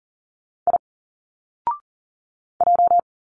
Morse code, heard as a keyed tone, about 40 wpm, 700 Hz. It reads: I J
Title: Single Letters 40wpm